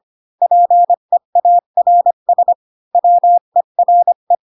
Morse code, heard as a keyed tone, about 25 wpm, 700 Hz. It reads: PEARS WERE
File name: Pinocchio / Pinocchio ch7 20 25 ik1hge